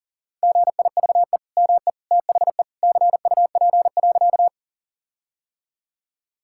Morse code, heard as a keyed tone, about 37 wpm, 700 Hz. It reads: GIVE ME THE CUP.
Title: Sentences from Top 1600 Words 40wpm